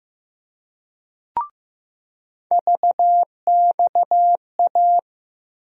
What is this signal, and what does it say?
Morse code, heard as a keyed tone, about 15 wpm, 700 Hz. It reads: VXA
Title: USA Call Sign Postfixes - 3 Letters 15wpm